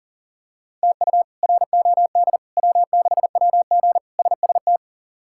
Morse code, heard as a keyed tone, about 40 wpm, 700 Hz. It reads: TU ROD W6WG SST
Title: K1USN SST QSO Practice - 40wpm